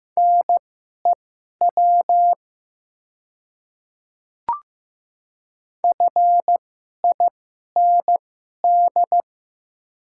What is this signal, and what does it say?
Morse code, heard as a keyed tone, about 15 wpm, spaced as for 11 wpm, 700 Hz. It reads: NEW FIND
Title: 2x Character Spacing - Top 300 Words - 15wpm